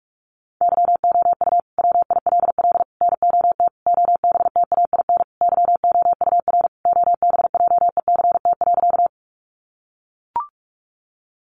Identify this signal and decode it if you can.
Morse code, heard as a keyed tone, about 44 wpm, 700 Hz. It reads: YOU WILL NOT OBTAIN YOUR OBJECT.
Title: Sentences from Top 1300 Words 45wpm